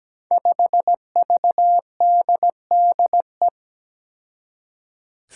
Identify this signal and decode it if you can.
Morse code, heard as a keyed tone, about 17 wpm, 700 Hz. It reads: 5VDDE
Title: Five Challenging Letter-Number Combinations 17wpm